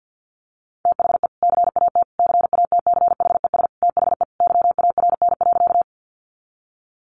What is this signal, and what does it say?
Morse code, heard as a keyed tone, about 50 wpm, 700 Hz. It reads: THE CAT CATCHES THE YARN.
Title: Additional Easy Phrases 50wpm